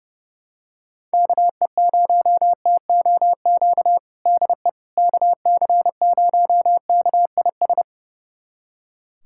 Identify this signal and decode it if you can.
Morse code, heard as a keyed tone, about 32 wpm, 700 Hz. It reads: KE0TOQ DE KC0KIS
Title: USA Callsign DE Callsign - 6 Characters - 30wpm